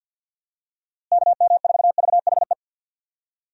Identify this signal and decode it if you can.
Morse code, heard as a keyed tone, about 52 wpm, 700 Hz. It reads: KM4VHE
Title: USA Call Signs - 6 Characters - Repeated 2x - 50wpm